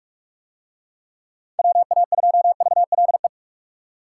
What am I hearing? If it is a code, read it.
Morse code, heard as a keyed tone, about 44 wpm, 700 Hz. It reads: WA2VLE